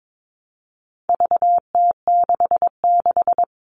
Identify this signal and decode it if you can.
Morse code, heard as a keyed tone, about 22 wpm, 700 Hz. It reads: VT66
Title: All Cut Numbers - 4-Digit 22wpm